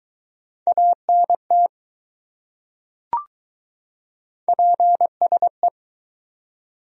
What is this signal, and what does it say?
Morse code, heard as a keyed tone, about 21 wpm, 700 Hz. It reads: ANT PSE